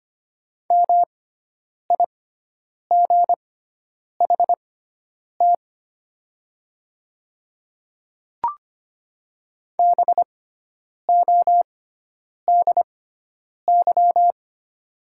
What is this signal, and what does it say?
Morse code, heard as a keyed tone, about 25 wpm, spaced as for 9 wpm, 700 Hz. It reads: MIGHT BODY
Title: 6x Character Spacing - Top 500 Words - 25wpm